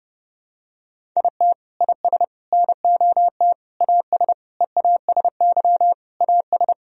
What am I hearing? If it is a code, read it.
Morse code, heard as a keyed tone, about 30 wpm, 700 Hz. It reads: IT IS NOT AS EASY AS
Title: Sentences from Top 1000 Words - Review Part 1 - 30wpm